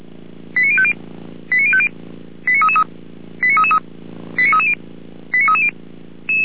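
Five-tone selective call (ZVEI), heard as quick runs of five tones, tones of about 70 ms. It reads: 89050 89050 89333 89330 89330
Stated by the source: Five-tone ZVEI selective call 89315, received by SDR, feuerwehr-orlamuende fire brigade.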